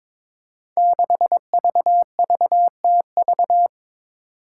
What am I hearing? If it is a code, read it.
Morse code, heard as a keyed tone, about 22 wpm, 700 Hz. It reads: 6VVTV
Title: Five Challenging Letter-Number Combinations 22wpm